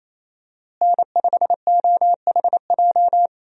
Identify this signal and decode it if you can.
Morse code, heard as a keyed tone, about 28 wpm, 700 Hz. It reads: N5OHJ
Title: USA Call Signs - 5 Characters - Repeated 3x - 28wpm